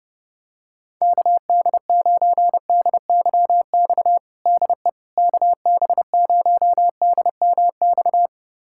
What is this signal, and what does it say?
Morse code, heard as a keyed tone, about 32 wpm, 700 Hz. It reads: KD9DYX DE KB0DMX
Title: USA Callsign DE Callsign 30wpm